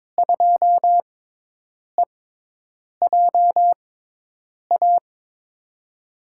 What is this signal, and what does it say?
Morse code, heard as a keyed tone, about 22 wpm, spaced as for 8 wpm, 700 Hz. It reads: 2EJA
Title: USA Call Signs - 5 Characters - 6x Character Spacing - 22wpm